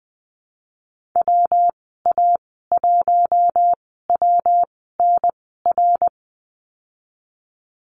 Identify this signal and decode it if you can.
Morse code, heard as a keyed tone, about 20 wpm, spaced as for 15 wpm, 700 Hz. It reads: WA1WNR
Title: 122 Lesson 20g - Number 3 - Call Signs 20wpm